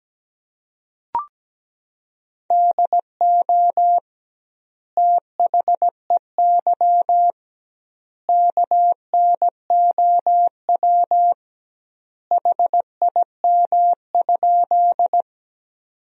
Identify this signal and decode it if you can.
Morse code, heard as a keyed tone, about 17 wpm, 700 Hz. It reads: DO THEY KNOW HIM?
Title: 2x Word-Spacing - Sentences from Top 100 Words - 17wpm